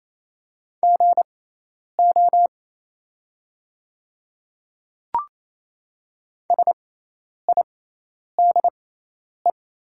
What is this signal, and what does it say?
Morse code, heard as a keyed tone, about 27 wpm, spaced as for 10 wpm, 700 Hz. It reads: GO SIDE
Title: 6x Character Spacing - Top 100 Words - 28wpm